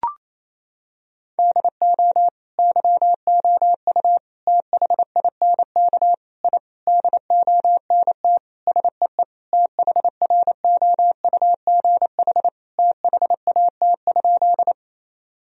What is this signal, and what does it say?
Morse code, heard as a keyed tone, about 28 wpm, 700 Hz. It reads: DO YOU THINK I DONT SEE THROUGH THAT?
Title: Sentences from Top 1000 Words - Review Part 1 - 28wpm